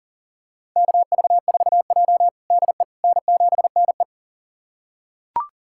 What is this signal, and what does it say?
Morse code, heard as a keyed tone, about 40 wpm, 700 Hz. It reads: KV4J DE N7NE
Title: USA Callsign DE Callsign - 4 Characters - 40wpm